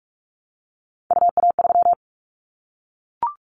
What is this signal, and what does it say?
Morse code, heard as a keyed tone, about 44 wpm, 700 Hz. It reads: UA3
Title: Callsign Prefixes for Top 100 DXCC - 45wpm